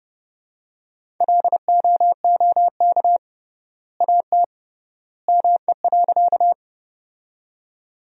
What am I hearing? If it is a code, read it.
Morse code, heard as a keyed tone, about 31 wpm, 700 Hz. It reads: LOOK AT ME.
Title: 3x Word-Spacing - Sentences from Top 200 Words - 30wpm